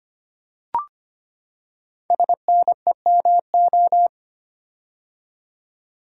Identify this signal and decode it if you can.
Morse code, heard as a keyed tone, about 25 wpm, 700 Hz. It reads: SNEMO
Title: Code Groups - 5 Letters - 25wpm